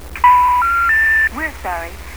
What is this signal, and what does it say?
Special information tone, with speech.